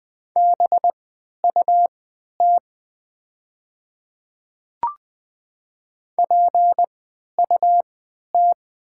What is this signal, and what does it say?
Morse code, heard as a keyed tone, about 20 wpm, spaced as for 11 wpm, 700 Hz. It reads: BUT PUT
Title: 3x Character Spacing - Top 500 Words - 20wpm